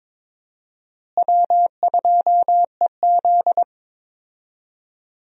Morse code, heard as a keyed tone, about 22 wpm, 700 Hz. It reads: W2EZ
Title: USA Call Signs - 4 Characters - Repeated 3x - 22wpm